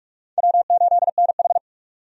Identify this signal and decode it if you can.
Morse code, heard as a keyed tone, about 46 wpm, 700 Hz. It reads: W8NH
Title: USA Call Signs - 4 Characters - 45wpm